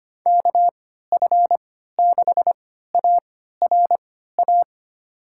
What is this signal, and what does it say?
Morse code, heard as a keyed tone, about 25 wpm, spaced as for 14 wpm, 700 Hz. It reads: KF6ARA
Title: USA Call Signs - 6 Characters - 3x Character Spacing - 25wpm